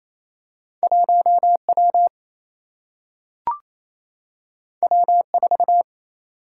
Morse code, heard as a keyed tone, about 28 wpm, 700 Hz. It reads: 1 W W 4
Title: Single-Letter Single-Number 28wpm